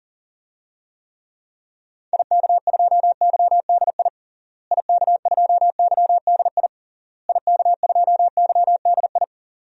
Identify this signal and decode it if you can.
Morse code, heard as a keyed tone, about 37 wpm, 700 Hz. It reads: IK2YDI IK2YDI IK2YDI
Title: DXCC-100 Call Signs - 6 Characters - Repeated 3x - 40wpm